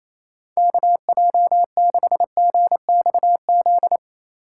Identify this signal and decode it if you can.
Morse code, heard as a keyed tone, about 27 wpm, 700 Hz. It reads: KJ6GXZ
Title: USA Call Signs - 3x Character Spacing - 28wpm